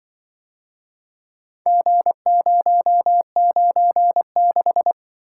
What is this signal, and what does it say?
Morse code, heard as a keyed tone, about 25 wpm, 700 Hz. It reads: G096